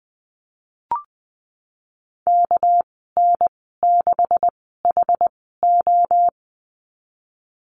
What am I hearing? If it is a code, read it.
Morse code, heard as a keyed tone, about 19 wpm, spaced as for 14 wpm, 700 Hz. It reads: KN6HO
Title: 213 Lesson 34g - Number 8 - Call Signs 20wpm